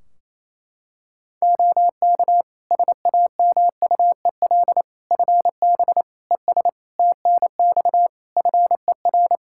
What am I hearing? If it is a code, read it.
Morse code, heard as a keyed tone, about 28 wpm, 700 Hz. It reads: OK SAMUEL FB ES TNX FER
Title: Practice QSO - Exchange 2 - Info - 28wpm